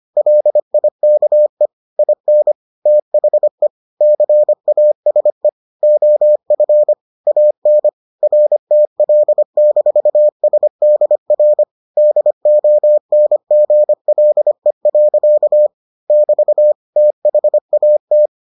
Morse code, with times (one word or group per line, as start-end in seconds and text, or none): 0.2-1.7 LIKE
2.0-2.5 IN
2.9-3.7 THE
4.0-5.5 CASE
5.8-6.9 OF
7.3-7.9 AN
8.2-11.6 RTL-SDR
12.0-15.7 DONGLE.
16.1-16.7 <BT>
17.0-18.3 THAT